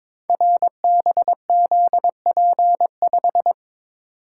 Morse code, heard as a keyed tone, about 22 wpm, 690 Hz. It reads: RBZP5
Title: Code Groups - 5 Letter:Numbers - 22wpm